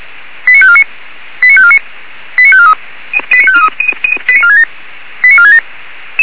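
Five-tone selective call (ZVEI), heard as quick runs of five tones, tones of about 70 ms. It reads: 89549 89549 89543 89543 89467 89467
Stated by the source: ZVEI five-tone call 89248 received by SDR, channel 464.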